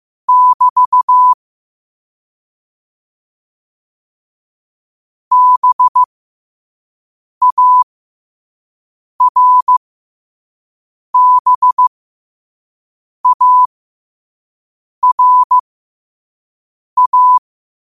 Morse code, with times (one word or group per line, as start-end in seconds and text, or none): 0.3-1.3 <BT>
5.3-17.4 BARBARA